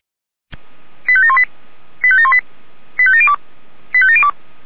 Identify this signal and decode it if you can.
Five-tone selective call (ZVEI), heard as quick runs of five tones, tones of about 70 ms. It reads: 85718 85718 85702 85702